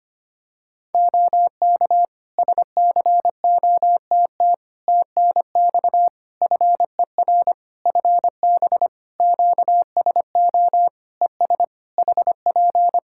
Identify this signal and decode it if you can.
Morse code, heard as a keyed tone, about 25 wpm, 700 Hz. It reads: OK SCOTT TNX FER FB QSO ES HP
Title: Practice QSO - Exchange 4 - Ending - 25wpm